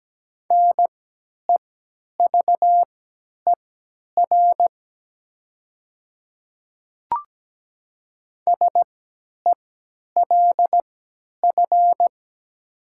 Morse code, heard as a keyed tone, about 17 wpm, spaced as for 10 wpm, 700 Hz. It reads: NEVER SELF
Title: 3x Character Spacing - Top 500 Words - 17wpm